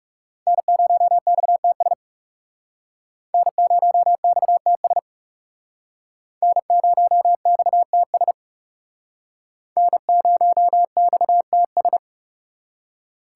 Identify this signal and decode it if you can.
Morse code, heard as a keyed tone, about 38 wpm, 700 Hz. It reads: N0XTS N0XTS N0XTS N0XTS
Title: Speed Racing - USA 1x3 Call Signs 30wpm